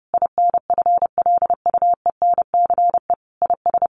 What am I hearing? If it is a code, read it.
Morse code, heard as a keyed tone, about 30 wpm, 700 Hz. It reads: INFLUENCE IS
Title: Sentences from Top 1100 Words 30wpm